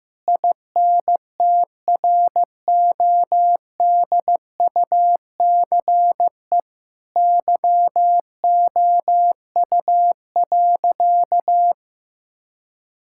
Morse code, heard as a keyed tone, about 16 wpm, 700 Hz. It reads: INTRODUCE YOU.
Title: Sentences from Top 1600 Words 15wpm